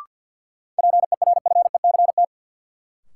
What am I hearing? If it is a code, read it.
Morse code, heard as a keyed tone, about 52 wpm, 700 Hz. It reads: PERFECT
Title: Sentences from Top 1000 Words - New Word Drill - 50wpm